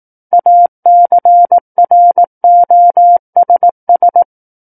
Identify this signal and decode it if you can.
Morse code, heard as a keyed tone, about 18 wpm, 700 Hz. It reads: ACROSS
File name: 18 WPM